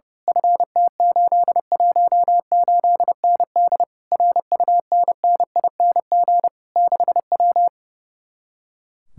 Morse code, heard as a keyed tone, about 28 wpm, 700 Hz. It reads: FT818ND RUNNING 6W
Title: Speed-Racing Common QSO Phrases - 30wpm